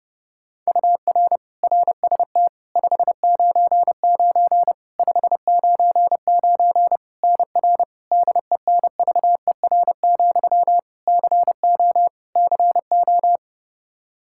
Morse code, heard as a keyed tone, about 30 wpm, 700 Hz. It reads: UR RST 599 599 NR DENVER, CO CO
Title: Practice QSO - Exchange 1 - Beginning - 30wpm